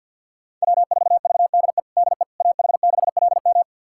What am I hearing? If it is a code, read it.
Morse code, heard as a keyed tone, about 50 wpm, 700 Hz. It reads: W4VDE DE AH6LM
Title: USA Callsign DE Callsign - 5 Characters - 50wpm